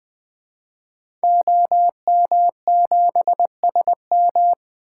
Morse code, heard as a keyed tone, about 20 wpm, 700 Hz. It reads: OM7SM
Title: 2023 SOTA Activator Call Signs - 20wpm